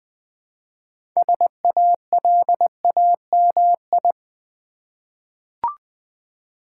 Morse code, heard as a keyed tone, about 20 wpm, 690 Hz. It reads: SALAMI
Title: Difficult Words with Hints - 20wpm